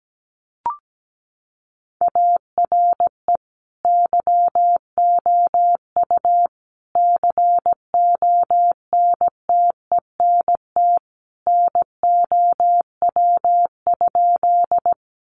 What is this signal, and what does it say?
Morse code, heard as a keyed tone, about 17 wpm, 700 Hz. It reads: ARE YOU CONTENT NOW?
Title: Sentences from Top 900 Words 17wpm